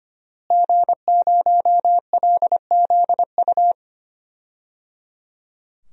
Morse code, heard as a keyed tone, about 25 wpm, 700 Hz. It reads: G0LZU
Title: DXCC-100 Call Signs - 5 Characters - Repeated 3x - 25wpm